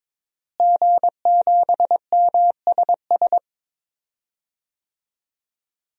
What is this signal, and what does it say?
Morse code, heard as a keyed tone, about 21 wpm, 700 Hz. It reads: G7MSS
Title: DXCC-100 Call Signs - 5 Characters - Repeated 2x - 22wpm